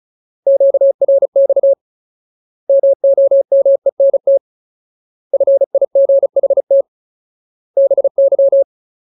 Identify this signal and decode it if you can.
Morse code, heard as a keyed tone, about 34 wpm, 550 Hz. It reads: QRX MOMENT FIGHT BY
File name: Ultimate Word List / 35WPM Ultimate